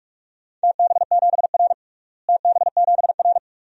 Morse code, heard as a keyed tone, about 41 wpm, 700 Hz. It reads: TB7R TB7R
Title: DXCC-100 Call Signs - 4 Characters - Repeated 3x - 45wpm